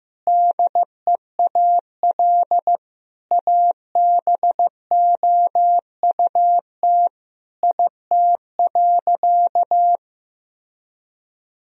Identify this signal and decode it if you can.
Morse code, heard as a keyed tone, about 16 wpm, 700 Hz. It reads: DEAL ABOUT IT.
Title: Sentences from Top 1000 Words - Review Part 1 - 15wpm